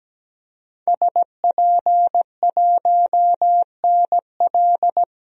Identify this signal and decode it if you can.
Morse code, heard as a keyed tone, about 17 wpm, 700 Hz. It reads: SP1NL
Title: DXCC-100 Call Signs - 5 Characters - 17wpm